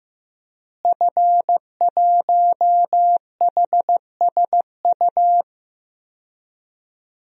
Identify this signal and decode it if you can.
Morse code, heard as a keyed tone, about 15 wpm, 700 Hz. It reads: F1HSU
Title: DXCC-100 Call Signs - 5 Characters - Repeated 2x - 15wpm